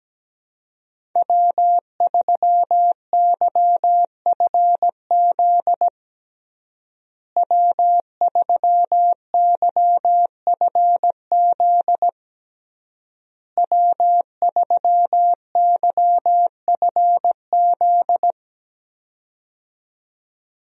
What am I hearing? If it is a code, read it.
Morse code, heard as a keyed tone, about 17 wpm, 700 Hz. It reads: W3YFZ W3YFZ W3YFZ
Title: USA Call Signs - 5 Characters - Repeated 3x - 17wpm